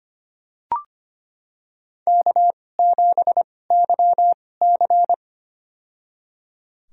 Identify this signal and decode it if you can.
Morse code, heard as a keyed tone, about 25 wpm, spaced as for 18 wpm, 700 Hz. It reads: K7YC
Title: USA Call Signs - 4 Characters - 2x Character Spacing - 25wpm